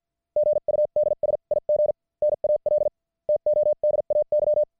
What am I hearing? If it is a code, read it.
Morse code, heard as a keyed tone, about 50 wpm, spaced as for 39 wpm, 600 Hz. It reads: GUD SIGNAL TODAY
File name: archived / CWBros